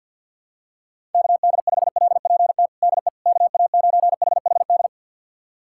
Morse code, heard as a keyed tone, about 50 wpm, 700 Hz. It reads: KD5LPT DE KA9HHD
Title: USA Callsign DE Callsign 50wpm